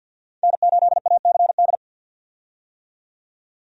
Morse code, heard as a keyed tone, about 48 wpm, 700 Hz. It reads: N8ACD
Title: USA Call Signs - 5 Characters - 50wpm